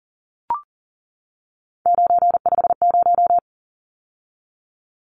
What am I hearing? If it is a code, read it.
Morse code, heard as a keyed tone, about 40 wpm, 700 Hz. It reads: 950
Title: Numbers - 3-Digit 40wpm